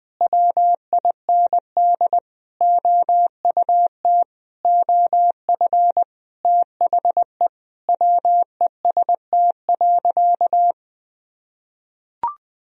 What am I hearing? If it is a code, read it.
Morse code, heard as a keyed tone, about 21 wpm, 700 Hz. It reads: WIND OUT OF THE WEST.
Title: Sentences from Top 1000 Words - Review Part 2 - 20wpm